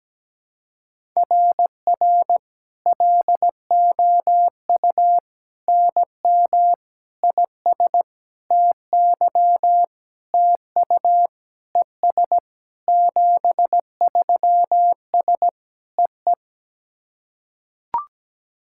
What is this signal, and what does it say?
Morse code, heard as a keyed tone, about 17 wpm, 700 Hz. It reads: RR LOU NM IS TY TU ES 73S EE
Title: SOTA QSO Practice 17wpm